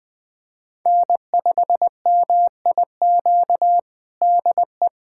Morse code, heard as a keyed tone, about 21 wpm, 700 Hz. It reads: N5MIQ DE
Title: USA Callsign DE Callsign 20wpm